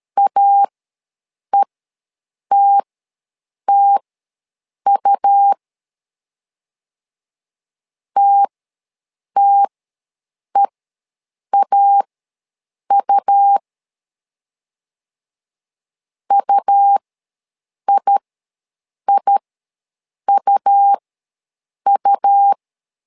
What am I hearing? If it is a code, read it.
Morse code, heard as a keyed tone, about 13 wpm, spaced as for 7 wpm, 800 Hz. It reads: AETTU TTEAU UIIUU
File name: LEZIONI / 2 UHDN / 2 U ws 12wpm 1655